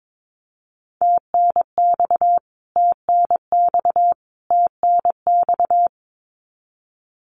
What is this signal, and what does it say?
Morse code, heard as a keyed tone, about 22 wpm, 700 Hz. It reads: TNX TNX TNX
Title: Abbreviations Repeated 3x - 22wpm